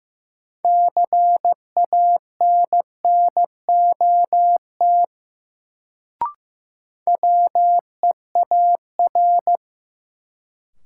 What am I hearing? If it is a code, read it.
Morse code, heard as a keyed tone, about 15 wpm, 700 Hz. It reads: CANNOT WEAR
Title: Sentences from Top 1000 Words - Review Part 6 - New Word Drill - 15wpm